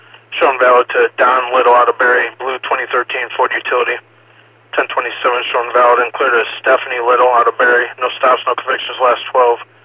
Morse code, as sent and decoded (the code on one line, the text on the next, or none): none